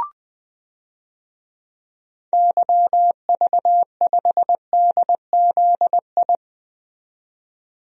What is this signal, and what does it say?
Morse code, heard as a keyed tone, about 20 wpm, 700 Hz. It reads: YV5DZI
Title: DXCC-100 Call Signs - 6 Characters - 20wpm